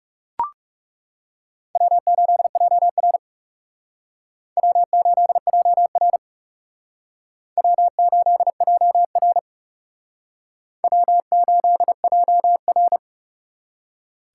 Morse code, heard as a keyed tone, about 36 wpm, 700 Hz. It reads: W8JR W8JR W8JR W8JR
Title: Speed Racing - USA Call Signs 30wpm